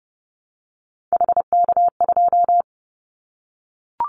Morse code, heard as a keyed tone, about 30 wpm, 700 Hz. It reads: HK2
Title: Callsign Prefixes for Top 100 DXCC - 30wpm